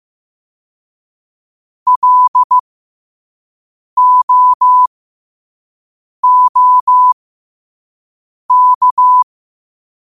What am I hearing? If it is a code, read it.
Morse code, heard as a keyed tone, about 15 wpm, spaced as for 5 wpm, 1000 Hz. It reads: LOOK
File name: qotdfiles / 5wpm